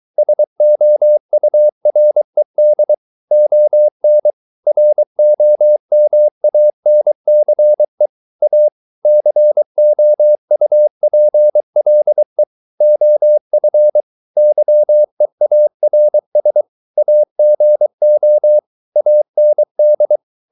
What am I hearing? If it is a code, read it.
Morse code, heard as a keyed tone, about 23 wpm, 600 Hz. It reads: SOURED ON ROMANCE A COUPLE OF YEARS AGO AND